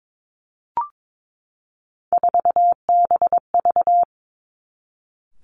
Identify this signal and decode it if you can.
Morse code, heard as a keyed tone, about 22 wpm, 700 Hz. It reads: 4BV
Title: Three Challenging Letter-Number Combinations 22wpm